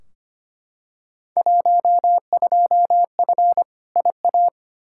Morse code, heard as a keyed tone, about 25 wpm, 700 Hz. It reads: 12F IA
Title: Field Day Exchanges Encore - 25wpm